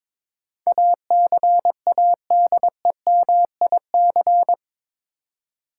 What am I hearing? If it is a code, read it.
Morse code, heard as a keyed tone, about 22 wpm, 700 Hz. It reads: ACADEMIC
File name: practice / 22wpm / Eight Letter Words - 22wpm